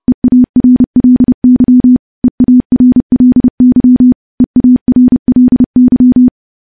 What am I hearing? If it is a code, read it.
Morse code, heard as a keyed tone, about 30 wpm, 250 Hz. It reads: EARLY EARLY EARLY
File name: morse / morse-training-30WPM